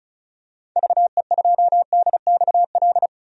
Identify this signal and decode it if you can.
Morse code, heard as a keyed tone, about 36 wpm, 690 Hz. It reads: VE2DXL
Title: DXCC-100 Call Signs - 6 Characters - 35wpm